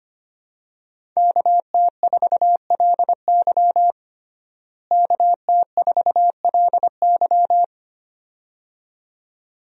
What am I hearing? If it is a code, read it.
Morse code, heard as a keyed tone, about 25 wpm, 700 Hz. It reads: KT4LY KT4LY
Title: USA Call Signs - 5 Characters - Repeated 2x - 25wpm